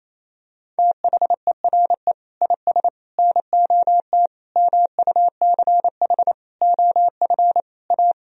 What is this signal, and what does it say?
Morse code, heard as a keyed tone, about 28 wpm, 700 Hz. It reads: THERE IS NOT MUCH OF A
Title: Sentences from Top 1000 Words - Review Part 4 - 28wpm